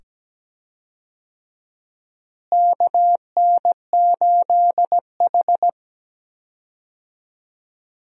Morse code, heard as a keyed tone, about 17 wpm, 700 Hz. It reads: KN8H
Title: USA Call Signs - 4 Characters - Repeated 2x - 17wpm